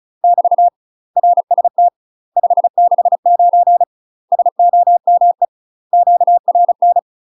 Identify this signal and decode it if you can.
Morse code, heard as a keyed tone, about 35 wpm, 700 Hz. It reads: <BT> RST 569 SOME QRN
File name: UK MASTER QSOs TRAINING / 35WPM / UK MASTER QSOs 0010